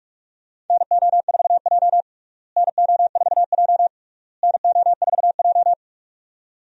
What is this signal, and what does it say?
Morse code, heard as a keyed tone, about 46 wpm, 700 Hz. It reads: NO4J NO4J NO4J
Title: USA Call Signs - 4 Characters - Repeated 3x - 45wpm